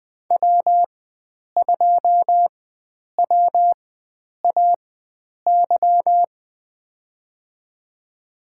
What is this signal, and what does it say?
Morse code, heard as a keyed tone, about 20 wpm, spaced as for 9 wpm, 700 Hz. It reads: W2WAY
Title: USA Call Signs - 4x Character Spacing - 20wpm